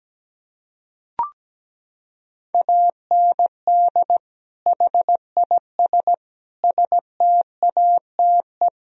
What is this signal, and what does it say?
Morse code, heard as a keyed tone, about 17 wpm, 700 Hz. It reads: AND HIS STATE
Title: Sentences from Top 1000 Words - Review Part 2 - 17wpm